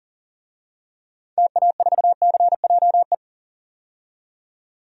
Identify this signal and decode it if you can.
Morse code, heard as a keyed tone, about 40 wpm, 700 Hz. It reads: TA4CJE
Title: DXCC-100 Call Signs - 6 Characters - 40wpm